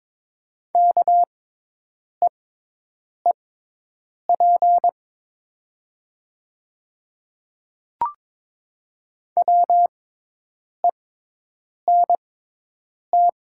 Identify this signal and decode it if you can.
Morse code, heard as a keyed tone, about 22 wpm, spaced as for 8 wpm, 690 Hz. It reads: KEEP WENT
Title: 6x Character Spacing - Top 300 Words - 22wpm